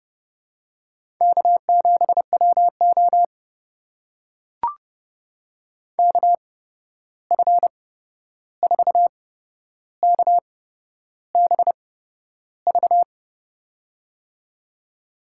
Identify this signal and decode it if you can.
Morse code, heard as a keyed tone, about 30 wpm, spaced as for 11 wpm, 700 Hz. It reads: K7WO KF4KBV
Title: USA Call Signs - 8x Character Spacing - 30wpm